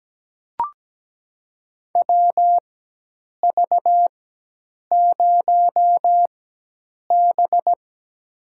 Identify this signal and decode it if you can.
Morse code, heard as a keyed tone, about 17 wpm, spaced as for 8 wpm, 700 Hz. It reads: WV0B
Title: USA Call Signs - 4 Characters - 4x Character Spacing - 17wpm